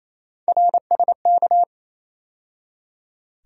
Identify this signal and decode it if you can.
Morse code, heard as a keyed tone, about 27 wpm, 700 Hz. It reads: RSK
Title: USA Call Sign Postfixes - 3 Letters 28wpm